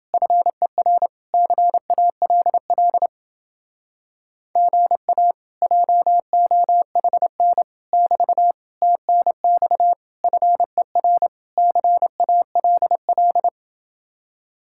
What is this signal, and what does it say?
Morse code, heard as a keyed tone, about 29 wpm, 700 Hz. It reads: FER CALL GA JOHN <BT> TNX FER CALL